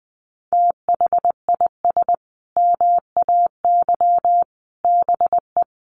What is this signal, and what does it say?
Morse code, heard as a keyed tone, about 20 wpm, 700 Hz. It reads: THIS MAY BE
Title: Sentences from Top 900 Words 20wpm